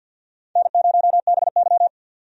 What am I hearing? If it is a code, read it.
Morse code, heard as a keyed tone, about 50 wpm, 700 Hz. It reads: N0BY
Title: USA Call Signs - 4 Characters - 50wpm